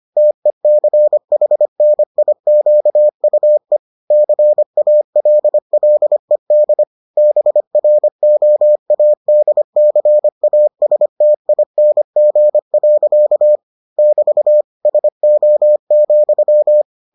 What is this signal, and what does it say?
Morse code, heard as a keyed tone, about 25 wpm, 600 Hz. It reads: TECHNIQUE CALLED BROADCASTING. <BT> SO,